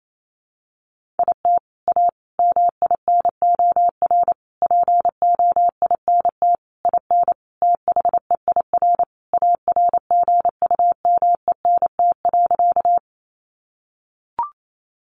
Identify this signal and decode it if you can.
Morse code, heard as a keyed tone, about 28 wpm, 700 Hz. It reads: IT A MINOR POINT IN THEIR ARGUMENT.
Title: Sentences from Top 1500 Words 28wpm